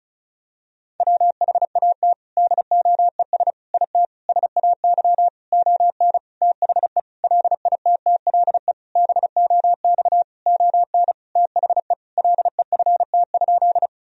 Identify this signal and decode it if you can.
Morse code, heard as a keyed tone, about 36 wpm, 700 Hz. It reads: WHAT DOES IT SAY ON THE LITTLE BOX ON THE LEFT?
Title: Sentences from Top 1000 Words - Review Part 2 - 35wpm